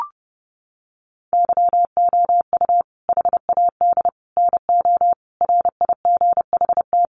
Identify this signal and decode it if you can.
Morse code, heard as a keyed tone, about 30 wpm, 700 Hz. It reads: YOU HAD NO RIGHT
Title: Sentences from Top 1300 Words 30wpm